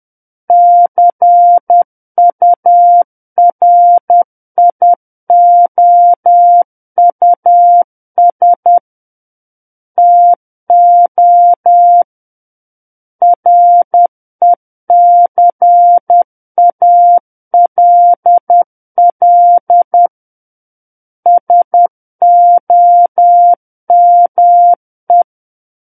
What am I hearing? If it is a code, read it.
Morse code, heard as a keyed tone, about 10 wpm, 700 Hz. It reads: CURIOUS TO RECALL SOME
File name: Book One / 01-Chapter 1 - 10 WPM